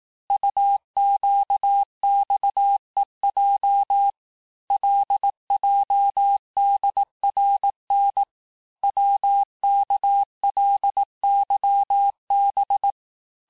Morse code, heard as a keyed tone, about 18 wpm, 800 Hz. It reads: UQXEJ LJDRN WKLYB
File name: cours-du-lundi files / cours-1594--teste-chiffre-vitesse-17mpm